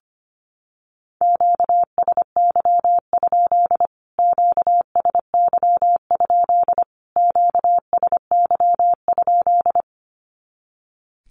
Morse code, heard as a keyed tone, about 25 wpm, 700 Hz. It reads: QSY? QSY? QSY?
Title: Q-Codes Repeated 3x 25wpm